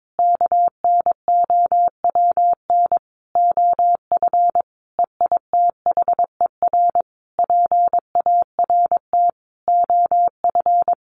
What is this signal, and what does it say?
Morse code, heard as a keyed tone, about 22 wpm, 700 Hz. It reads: KNOWN OF EITHER PART OF
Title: Sentences from Top 1500 Words 22wpm